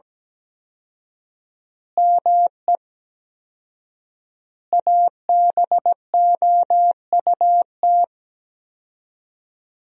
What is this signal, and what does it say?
Morse code, heard as a keyed tone, about 17 wpm, 700 Hz. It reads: ME ABOUT
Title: 4x Word-Spacing - Sentences from Top 200 Words - 17wpm